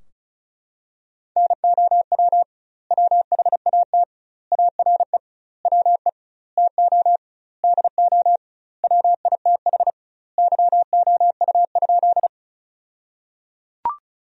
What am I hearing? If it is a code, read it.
Morse code, heard as a keyed tone, about 36 wpm, 700 Hz. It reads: NOW WHAT ARE WE TO DO WITH YOU?
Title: 2x Word-Spacing - Sentences from Top 100 Words - 35wpm